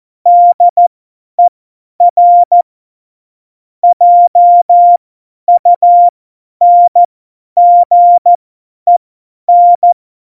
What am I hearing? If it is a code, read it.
Morse code, heard as a keyed tone, about 14 wpm, spaced as for 10 wpm, 700 Hz. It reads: DER JUNGEN
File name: Morsekurs / Witze / Witz 01. Dezember 2020, 50 F70